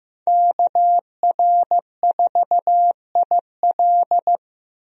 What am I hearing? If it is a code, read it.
Morse code, heard as a keyed tone, about 15 wpm, 700 Hz. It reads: KR4IL
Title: USA Call Signs - 5 Characters - 15wpm